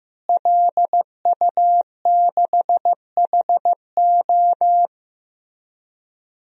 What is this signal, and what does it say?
Morse code, heard as a keyed tone, about 15 wpm, 700 Hz. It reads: LU6HO
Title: DXCC-100 Call Signs - 5 Characters - Repeated 3x - 15wpm